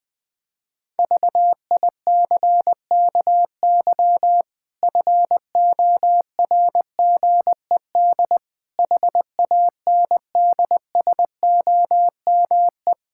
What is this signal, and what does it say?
Morse code, heard as a keyed tone, about 20 wpm, 700 Hz. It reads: VICKY FORGED HANDSOME
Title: Pangram Challenge v2 - 20wpm